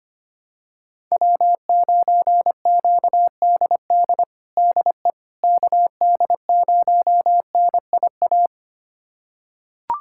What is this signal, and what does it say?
Morse code, heard as a keyed tone, about 25 wpm, 700 Hz. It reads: W9QDD DE KD0NIA
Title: USA Callsign DE Callsign 25wpm